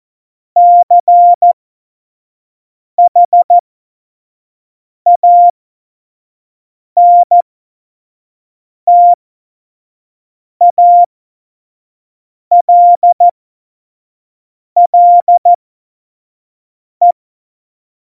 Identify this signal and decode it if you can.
Morse code, heard as a keyed tone, about 14 wpm, spaced as for 5 wpm, 700 Hz. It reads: CHANTALLE
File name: Morsekurs / Witze / Witz 01. Dezember, 2020 25 F70